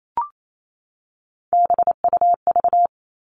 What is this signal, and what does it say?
Morse code, heard as a keyed tone, about 28 wpm, 700 Hz. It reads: BUV
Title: All Cut Numbers - 3-Digit 28wpm